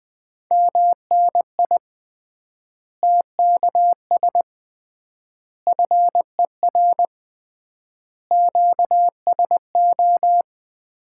Morse code, heard as a keyed tone, about 20 wpm, 700 Hz. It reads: MNI TKS FER QSO
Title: Common QSO Phrases - 3x Word-Spacing - 20wpm